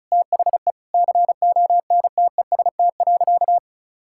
Morse code, heard as a keyed tone, about 36 wpm, 700 Hz. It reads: THE CONTEST.
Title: Sentences from Top 1600 Words 35wpm